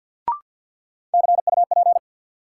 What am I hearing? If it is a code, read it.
Morse code, heard as a keyed tone, about 52 wpm, 700 Hz. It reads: CUP
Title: Sentences from Top 1600 Words - New Word Drill - 50wpm